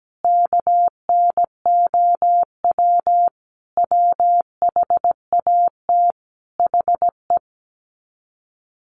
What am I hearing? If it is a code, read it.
Morse code, heard as a keyed tone, about 17 wpm, 700 Hz. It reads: KNOW WHAT HE
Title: Sets of 5 Words Encore2 17wpm